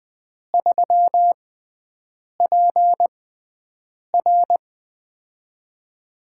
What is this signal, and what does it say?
Morse code, heard as a keyed tone, about 20 wpm, spaced as for 7 wpm, 700 Hz. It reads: 3PR
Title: USA Call Signs - 5 Characters - 6x Character Spacing - 20wpm